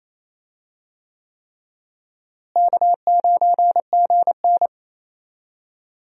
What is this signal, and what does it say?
Morse code, heard as a keyed tone, about 28 wpm, 700 Hz. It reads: K9GN
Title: USA Call Signs - 4 Characters - Repeated 3x - 28wpm